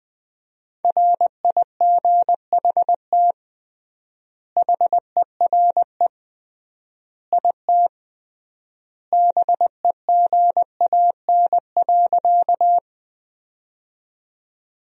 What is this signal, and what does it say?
Morse code, heard as a keyed tone, about 21 wpm, 700 Hz. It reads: RIGHT HERE IT BEGAN.
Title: 3x Word-Spacing - Sentences from Top 300 Words - 20wpm